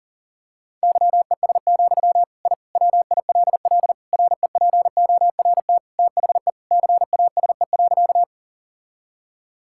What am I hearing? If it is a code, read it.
Morse code, heard as a keyed tone, about 40 wpm, 700 Hz. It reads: YES, I WILL REPORT THE CASE.
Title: Sentences from Top 1000 Words - Review Part 6 - 40wpm